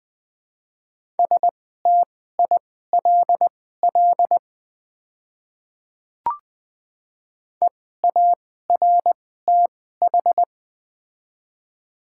Morse code, heard as a keyed tone, about 21 wpm, spaced as for 15 wpm, 700 Hz. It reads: STILL EARTH
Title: 2x Character Spacing - Top 500 Words - 20wpm